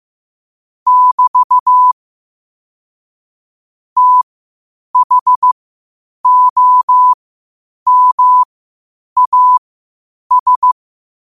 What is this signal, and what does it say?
Morse code, heard as a keyed tone, about 15 wpm, spaced as for 8 wpm, 1000 Hz. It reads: <BT> THOMAS
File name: qotdfiles / 8wpm